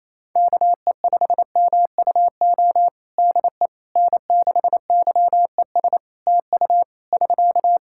Morse code, heard as a keyed tone, about 28 wpm, 700 Hz. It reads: KE5MUO DE N6YES TU <SK>
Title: K1USN SST QSO Practice 28wpm